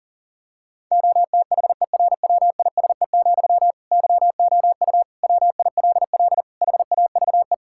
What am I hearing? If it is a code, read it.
Morse code, heard as a keyed tone, about 40 wpm, 700 Hz. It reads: OTHERWISE, YOU WILL HAVE
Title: Sentences from Top 1600 Words 40wpm